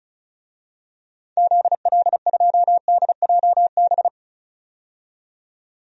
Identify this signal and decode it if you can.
Morse code, heard as a keyed tone, about 34 wpm, 700 Hz. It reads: ZL2DJB
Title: DXCC-100 Call Signs - 6 Characters - 35wpm